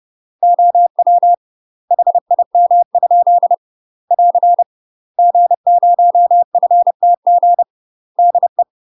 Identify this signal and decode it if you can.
Morse code, heard as a keyed tone, about 30 wpm, 700 Hz. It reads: OW HIM? <AR> G0FTG DE
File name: UK MASTER QSOs TRAINING / 30WPM / UK MASTER QSOs 0040